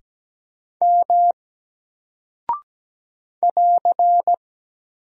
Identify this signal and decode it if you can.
Morse code, heard as a keyed tone, about 17 wpm, 700 Hz. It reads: M <AR>
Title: Single Character (Letter:Number:Punctuation:Prosign) 17wpm